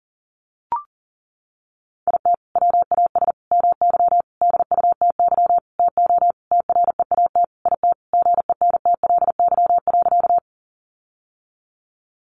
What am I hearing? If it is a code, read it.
Morse code, heard as a keyed tone, about 40 wpm, 700 Hz. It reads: IT WAS MY DUTY TO TREAT IT GENTLY.
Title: Sentences from Top 1500 Words 40wpm